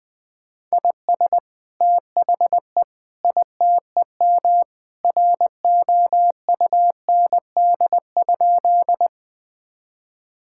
Morse code, heard as a keyed tone, about 20 wpm, 700 Hz. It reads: IS THE ITEM ROUND?
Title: Sentences from Top 1000 Words - Review Part 6 - 20wpm